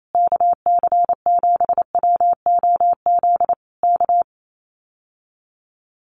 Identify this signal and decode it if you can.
Morse code, heard as a keyed tone, about 28 wpm, 700 Hz. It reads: KC7WOZ K
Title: Common QSO Phrases Encore 28wpm